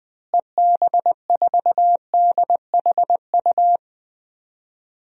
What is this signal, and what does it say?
Morse code, heard as a keyed tone, about 20 wpm, 700 Hz. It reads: EB4DHU
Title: DXCC-100 Call Signs - 6 Characters - Repeated 2x - 20wpm